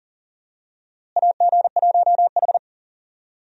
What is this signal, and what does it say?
Morse code, heard as a keyed tone, about 40 wpm, 700 Hz. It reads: AG1H
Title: USA Call Signs - 4 Characters - Repeated 2x - 40wpm